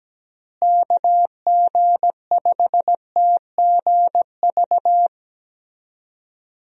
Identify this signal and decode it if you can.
Morse code, heard as a keyed tone, about 17 wpm, 700 Hz. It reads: KG5TGV
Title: USA Call Signs - 6 Characters - Repeated 3x - 17wpm